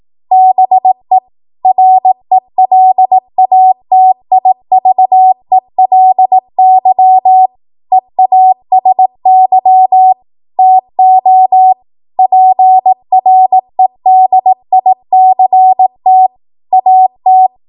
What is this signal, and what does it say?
Morse code, heard as a keyed tone, about 18 wpm, 750 Hz. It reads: BE RELATIVELY EASY TO PREDICT AT
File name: Morse / 18WPM